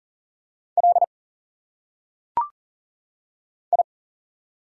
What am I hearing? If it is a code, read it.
Morse code, heard as a keyed tone, about 39 wpm, 700 Hz. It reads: L I
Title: Single Letters - Phonetic Pronunciation 40wpm